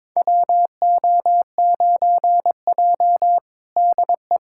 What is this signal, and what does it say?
Morse code, heard as a keyed tone, about 22 wpm, 700 Hz. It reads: WO9J DE
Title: USA Callsign DE Callsign - 4 Characters - 22wpm